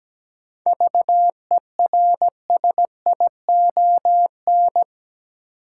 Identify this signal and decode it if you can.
Morse code, heard as a keyed tone, about 17 wpm, 700 Hz. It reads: VERSION
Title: Sentences from Top 1400 Words - New Word Drill - 17wpm